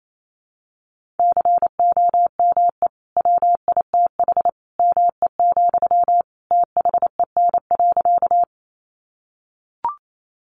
Morse code, heard as a keyed tone, about 28 wpm, 700 Hz. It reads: COME WITH ME, THEN.
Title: Sentences from Top 200 Words 28wpm